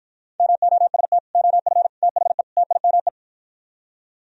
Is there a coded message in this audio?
Yes: Morse code from 0.4 to 3.1 s, heard as a keyed tone, about 56 wpm, 700 Hz. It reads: MOST OF THE TIME